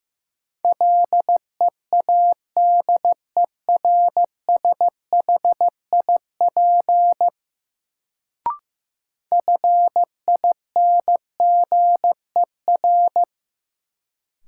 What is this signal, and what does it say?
Morse code, heard as a keyed tone, about 15 wpm, 700 Hz. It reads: LEADERSHIP FINGER
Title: Sentences from Top 1600 Words - New Word Drill Alt - 15wpm